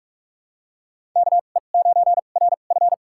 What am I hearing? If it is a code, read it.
Morse code, heard as a keyed tone, about 46 wpm, spaced as for 33 wpm, 700 Hz. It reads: KE9RF
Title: USA Call Signs - 5 Characters - 2x Character Spacing - 45wpm